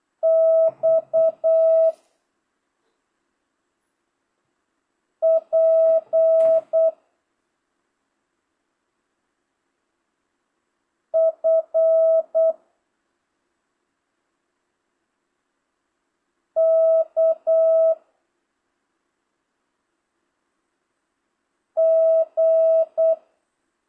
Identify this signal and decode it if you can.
Morse code, heard as a keyed tone, about 8 wpm, 650 Hz. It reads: X P F K G
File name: jh8cbh / 2025cw